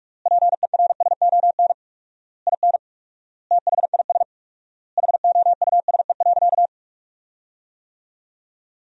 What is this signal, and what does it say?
Morse code, heard as a keyed tone, about 46 wpm, 700 Hz. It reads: PERSON IN THIS HOUSE.
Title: 4x Word-Spacing - Sentences from Top 400 Words - 45wpm